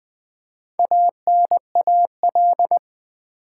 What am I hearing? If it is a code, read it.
Morse code, heard as a keyed tone, about 20 wpm, 700 Hz. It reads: ANAL
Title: Build A Word - 2 Letters At A Time - 8-Letter Words - 20wpm